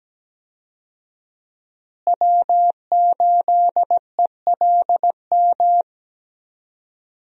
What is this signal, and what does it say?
Morse code, heard as a keyed tone, about 17 wpm, 700 Hz. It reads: W8ELM
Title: USA Call Signs - 5 Characters - Repeated 2x - 17wpm